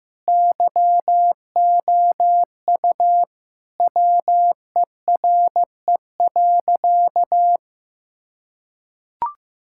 Morse code, heard as a keyed tone, about 15 wpm, 700 Hz. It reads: YOU WERE.
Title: Sentences from Top 1000 Words - Review Part 6 - 15wpm